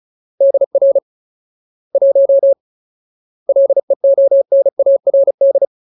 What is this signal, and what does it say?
Morse code, heard as a keyed tone, about 34 wpm, 550 Hz. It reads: DR 1 LEONARD
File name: Ultimate Word List / 35WPM Ultimate